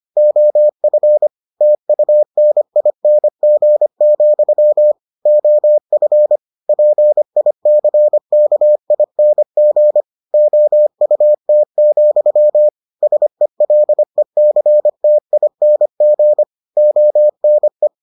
Morse code, heard as a keyed tone, about 25 wpm, 600 Hz. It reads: OF TUNING, OF PICKING OUT, SELECTING ONE